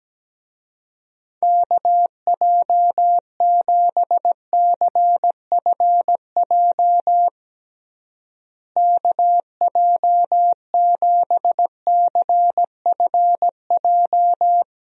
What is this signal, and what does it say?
Morse code, heard as a keyed tone, about 17 wpm, 700 Hz. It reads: KJ7CFJ KJ7CFJ
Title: USA Call Signs - 6 Characters - Repeated 2x - 17wpm